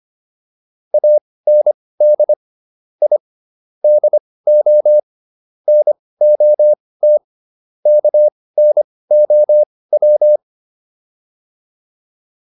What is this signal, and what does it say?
Morse code, heard as a keyed tone, about 25 wpm, spaced as for 18 wpm, 600 Hz. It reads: AND I DO NOT KNOW